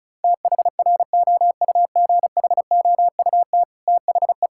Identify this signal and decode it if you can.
Morse code, heard as a keyed tone, about 36 wpm, 700 Hz. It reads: THROUGHOUT THE
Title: Sentences from Top 1000 Words - Review Part 5 - 35wpm